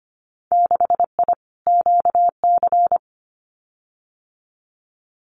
Morse code, heard as a keyed tone, about 25 wpm, 690 Hz. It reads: 6I QC
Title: Winter Field Day Exchanges Encore 25wpm